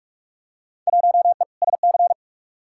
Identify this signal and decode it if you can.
Morse code, heard as a keyed tone, about 46 wpm, 700 Hz. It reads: 1E SC
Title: Field Day Exchanges - 45wpm